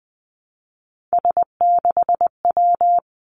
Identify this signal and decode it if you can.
Morse code, heard as a keyed tone, about 20 wpm, 700 Hz. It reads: S6W
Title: Three Challenging Letter-Number Combinations 20wpm